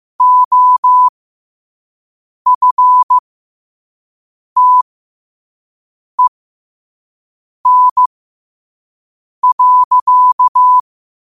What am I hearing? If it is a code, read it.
Morse code, heard as a keyed tone, about 15 wpm, spaced as for 5 wpm, 1000 Hz. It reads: OFTEN.
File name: qotdfiles / 5wpm